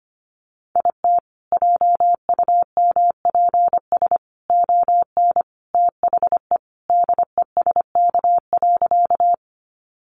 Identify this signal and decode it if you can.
Morse code, heard as a keyed tone, about 25 wpm, 700 Hz. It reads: IT JUMPS ON THE DESK.
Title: Additional Easy Phrases 25wpm